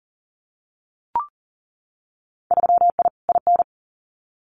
Morse code, heard as a keyed tone, about 39 wpm, 700 Hz. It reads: 3I IN
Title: Winter Field Day Exchanges 40wpm